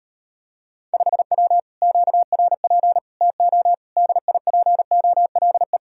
Morse code, heard as a keyed tone, about 38 wpm, 700 Hz. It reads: 5W QRP TO DIPOLE